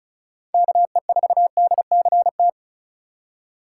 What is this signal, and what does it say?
Morse code, heard as a keyed tone, about 34 wpm, 700 Hz. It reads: KE4DCT
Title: USA Call Signs - 6 Characters - 4x Character Spacing - 35wpm